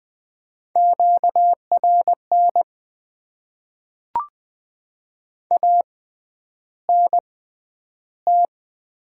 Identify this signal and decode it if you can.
Morse code, heard as a keyed tone, about 20 wpm, spaced as for 10 wpm, 700 Hz. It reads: QRN ANT
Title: Top 90 QSO Elements - 6x Character Spacing - 20wpm